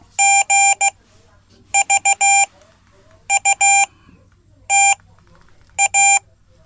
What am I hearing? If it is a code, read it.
Morse code, heard as a keyed tone, about 16 wpm, spaced as for 8 wpm, 790 Hz. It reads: GVUTA
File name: Code / 5min-c